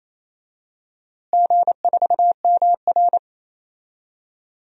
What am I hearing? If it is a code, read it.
Morse code, heard as a keyed tone, about 27 wpm, 700 Hz. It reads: G4MR
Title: DXCC-100 Call Signs - 4 Characters - Repeated 3x - 28wpm